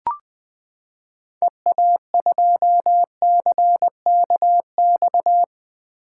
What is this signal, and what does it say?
Morse code, heard as a keyed tone, about 20 wpm, 700 Hz. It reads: EA2CKX
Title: 2023 SOTA Chaser Call Signs - 20wpm